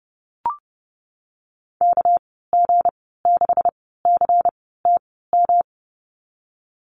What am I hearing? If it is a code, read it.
Morse code, heard as a keyed tone, about 30 wpm, spaced as for 17 wpm, 700 Hz. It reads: KG6CTM
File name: Lesson 37g - Letter x - Call Signs 30wpm